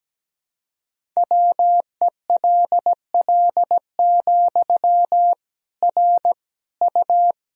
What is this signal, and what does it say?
Morse code, heard as a keyed tone, about 17 wpm, 700 Hz. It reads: WELL, R U
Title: Sentences from Top 400 Words and Abbreviations 17wpm